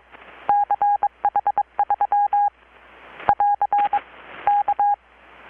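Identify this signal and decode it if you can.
Morse code, heard as a keyed tone, about 22 wpm, 820 Hz. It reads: CH3 <AR> K
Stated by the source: Morse from KPH's Night of Nights XX maritime radiotelegraphy event, received in Chiapas, Mexico about 3,800 km away on a Kenwood R-600 with a grounded 10 m long-wire antenna.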